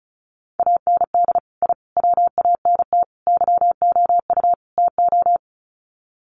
Morse code, heard as a keyed tone, about 34 wpm, 700 Hz. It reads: AND I WANT YOU TO
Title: Sets of 5 Words Encore 35wpm